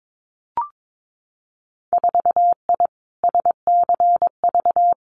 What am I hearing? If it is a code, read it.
Morse code, heard as a keyed tone, about 22 wpm, 700 Hz. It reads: 4I SCV
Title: Winter Field Day Exchanges Encore 22wpm